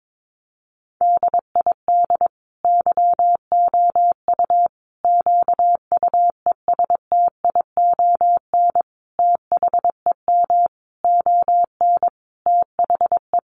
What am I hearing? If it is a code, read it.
Morse code, heard as a keyed tone, about 22 wpm, 700 Hz. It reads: DID YOU QUESTION THEM ON THE
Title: Sentences from Top 800 Words 22wpm